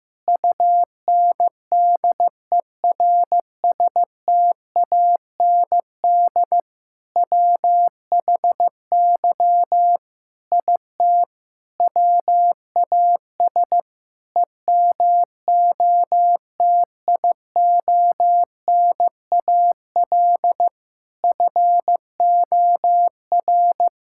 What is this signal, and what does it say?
Morse code, heard as a keyed tone, about 16 wpm, 690 Hz. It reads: UNDERSTAND WHY IT WAS EMOTIONAL FOR
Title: Sentences from Top 1000 Words - Review Part 5 - 15wpm